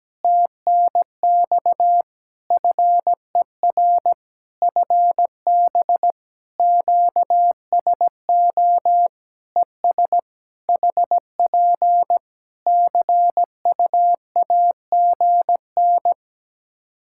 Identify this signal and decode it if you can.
Morse code, heard as a keyed tone, about 17 wpm, 700 Hz. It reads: TNX FER FB QSO ES HP CUAGN
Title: K1USN SST QSO Practice 17wpm